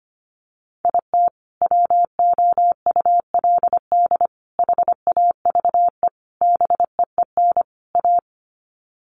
Morse code, heard as a keyed tone, about 25 wpm, 700 Hz. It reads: IT WOULD HAVE BEEN A
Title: Sets of 5 Words Encore 25wpm